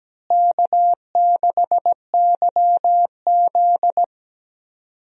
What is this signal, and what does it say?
Morse code, heard as a keyed tone, about 17 wpm, 700 Hz. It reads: K6YZ
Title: USA Call Signs - 4 Characters - Repeated 3x - 17wpm